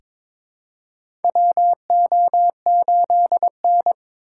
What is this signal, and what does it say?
Morse code, heard as a keyed tone, about 22 wpm, 700 Hz. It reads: WO8N
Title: USA Call Signs - 2x Character Spacing - 22wpm